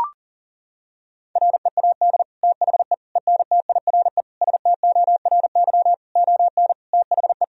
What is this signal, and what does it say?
Morse code, heard as a keyed tone, about 40 wpm, 700 Hz. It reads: READ THE ENTIRE STORY ON THE
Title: Sentences from Top 1000 Words - Review Part 4 - 40wpm